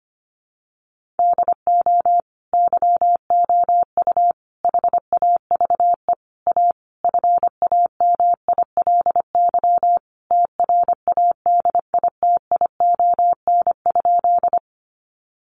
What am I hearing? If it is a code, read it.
Morse code, heard as a keyed tone, about 25 wpm, 700 Hz. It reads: DO YOU HAVE A FAMILY TRADITION?
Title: Sentences from Top 1500 Words 25wpm